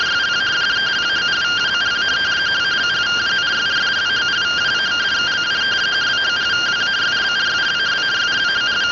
Two-tone teleprinter signal, about 170 Hz shift.